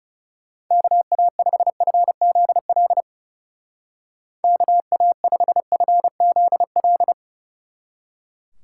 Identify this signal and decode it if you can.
Morse code, heard as a keyed tone, about 37 wpm, 700 Hz. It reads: KA5FZL KA5FZL
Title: Speed Racing - USA 2x3 Call Signs 30wpm